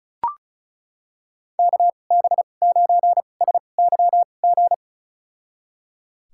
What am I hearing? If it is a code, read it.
Morse code, heard as a keyed tone, about 34 wpm, spaced as for 25 wpm, 700 Hz. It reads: KB9SYG
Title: USA Call Signs - 6 Characters - 2x Character Spacing - 35wpm